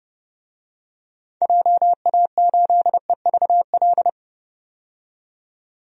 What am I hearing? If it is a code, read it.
Morse code, heard as a keyed tone, about 32 wpm, 700 Hz. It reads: JA8EVL
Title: DXCC-100 Call Signs - 6 Characters - 30wpm